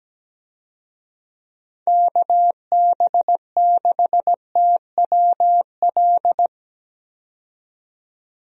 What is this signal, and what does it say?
Morse code, heard as a keyed tone, about 17 wpm, 700 Hz. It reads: KB6TWL
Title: USA Call Signs - 6 Characters - 17wpm